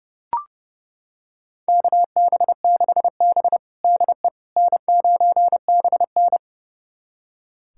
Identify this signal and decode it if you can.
Morse code, heard as a keyed tone, about 30 wpm, 700 Hz. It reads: KB6B DE N9BN
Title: USA Callsign DE Callsign - 4 Characters - 30wpm